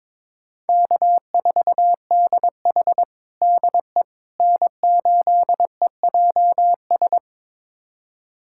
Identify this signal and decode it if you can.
Morse code, heard as a keyed tone, about 22 wpm, 700 Hz. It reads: K4DH DE N8EJS
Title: USA Callsign DE Callsign 22wpm